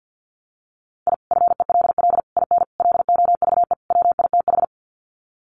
Morse code, heard as a keyed tone, about 49 wpm, 700 Hz. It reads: I FELL IN LOVE WITH
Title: Sets of 5 Words Encore 50wpm